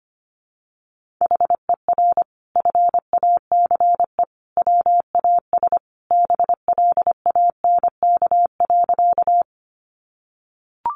Morse code, heard as a keyed tone, about 25 wpm, 700 Hz. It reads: HER FACE WAS BLANK.
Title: Sentences from Top 1500 Words 25wpm